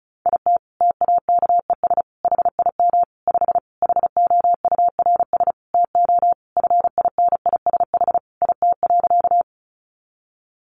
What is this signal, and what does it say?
Morse code, heard as a keyed tone, about 34 wpm, 700 Hz. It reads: IT TAKES HIM 5 HOURS TO FINISH IT.
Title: Additional Easy Phrases 35wpm